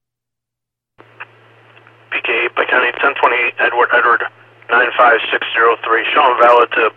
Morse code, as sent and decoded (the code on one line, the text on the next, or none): none